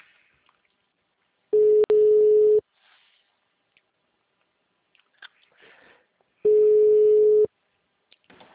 Ringback tone (Europe, 425 Hz).